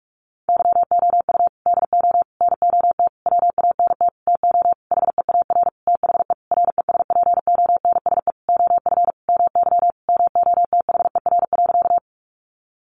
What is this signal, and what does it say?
Morse code, heard as a keyed tone, about 44 wpm, 700 Hz. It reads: YOU DO NOT WANT TO HEAR THE RESPONSE OF MY MOTHER.
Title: Sentences from Top 1400 Words 45wpm